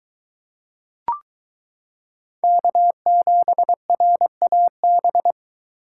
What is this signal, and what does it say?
Morse code, heard as a keyed tone, about 22 wpm, 700 Hz. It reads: K7RAB